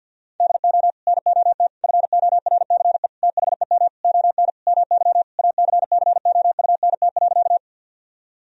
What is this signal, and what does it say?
Morse code, heard as a keyed tone, about 51 wpm, 700 Hz. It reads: DO NOT FORCE THEM ON MY ACCOUNT.
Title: Sentences from Top 1000 Words - Review Part 3 - 50wpm